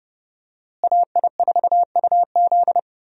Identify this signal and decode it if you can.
Morse code, heard as a keyed tone, about 30 wpm, 700 Hz. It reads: AI4UZ
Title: USA Call Signs - 5 Characters - Repeated 3x - 30wpm